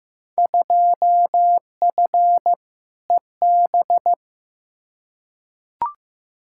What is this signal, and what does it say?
Morse code, heard as a keyed tone, about 15 wpm, 700 Hz. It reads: 2F EB
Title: Field Day Exchanges Encore - 15wpm